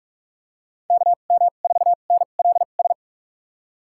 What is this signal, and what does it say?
Morse code, heard as a keyed tone, about 46 wpm, spaced as for 32 wpm, 700 Hz. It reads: KM4NLS
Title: USA Call Signs - 6 Characters - 2x Character Spacing - 45wpm